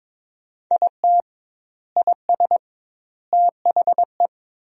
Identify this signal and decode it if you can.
Morse code, heard as a keyed tone, about 22 wpm, 700 Hz. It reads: IT IS THE
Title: 2x Word-Spacing - Sentences from Top 400 Words - 22wpm